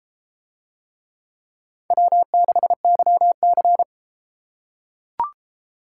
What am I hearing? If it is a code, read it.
Morse code, heard as a keyed tone, about 31 wpm, 700 Hz. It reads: W6YC